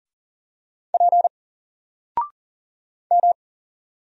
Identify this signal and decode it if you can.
Morse code, heard as a keyed tone, about 39 wpm, 700 Hz. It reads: P M
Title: Single Character (Letter:Number:Punctuation:Prosign) 40wpm